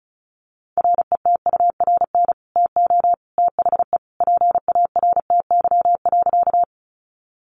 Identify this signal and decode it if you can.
Morse code, heard as a keyed tone, about 34 wpm, 700 Hz. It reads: RETURN TO THE PARTY.
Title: Sentences from Top 1400 Words 35wpm